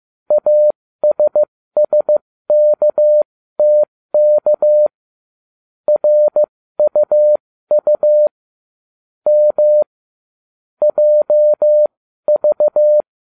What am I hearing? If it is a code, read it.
Morse code, heard as a keyed tone, about 15 wpm, spaced as for 13 wpm, 600 Hz. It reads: ASSKTK RUU M JV
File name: cw / ubacw20 2